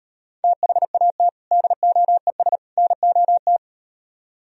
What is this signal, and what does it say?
Morse code, heard as a keyed tone, about 38 wpm, 700 Hz. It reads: THAT DOES NOT